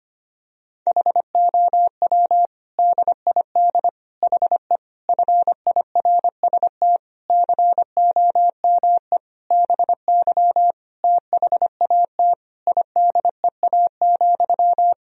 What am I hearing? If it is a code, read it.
Morse code, heard as a keyed tone, about 25 wpm, 700 Hz. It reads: HOW DID HE FIRST COME BY THAT IDEA,
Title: Sentences from Top 1000 Words - Review Part 2 - 25wpm